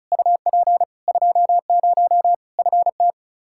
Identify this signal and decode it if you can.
Morse code, heard as a keyed tone, about 36 wpm, 700 Hz. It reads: UP 20 FT
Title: Practice QSO - Exchange 2 - Info - 35wpm